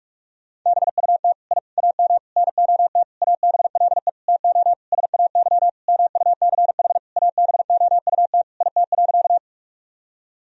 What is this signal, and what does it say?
Morse code, heard as a keyed tone, about 41 wpm, 700 Hz. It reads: BUT I AM NOT ABLE TO SAY MUCH ABOUT IT.
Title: Sentences from Top 1000 Words - Review Part 2 - 45wpm